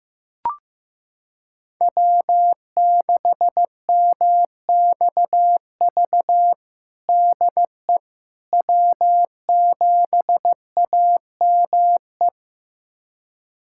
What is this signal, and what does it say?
Morse code, heard as a keyed tone, about 15 wpm, 700 Hz. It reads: W6MXV DE W7AME
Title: USA Callsign DE Callsign - 5 Characters - 15wpm